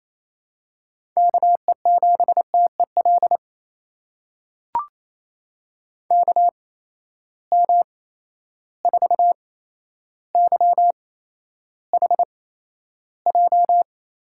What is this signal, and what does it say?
Morse code, heard as a keyed tone, about 28 wpm, spaced as for 12 wpm, 700 Hz. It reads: KE7TEL KM4YHJ
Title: USA Call Signs - 6 Characters - 8x Character Spacing - 28wpm